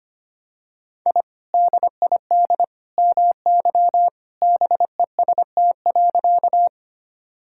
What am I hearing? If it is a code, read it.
Morse code, heard as a keyed tone, about 25 wpm, 690 Hz. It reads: I DID MY BEST.
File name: practice / 25wpm / Sentences with 2-Word Phrases - 25wpm